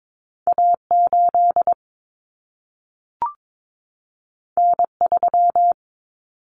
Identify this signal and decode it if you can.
Morse code, heard as a keyed tone, about 22 wpm, 700 Hz. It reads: A8 N3
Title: Common Cut Numbers - 2-Digit 22wpm